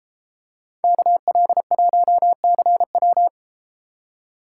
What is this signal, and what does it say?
Morse code, heard as a keyed tone, about 32 wpm, 700 Hz. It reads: KL1CW